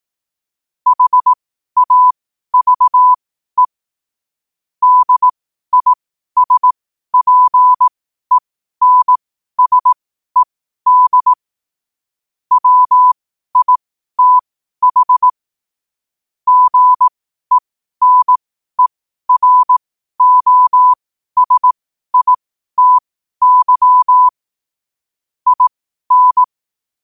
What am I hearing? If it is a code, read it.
Morse code, heard as a keyed tone, about 18 wpm, spaced as for 12 wpm, 1000 Hz. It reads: HAVE DISPENSED WITH GENEROSITY IN
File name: 2025-11-08-12wpm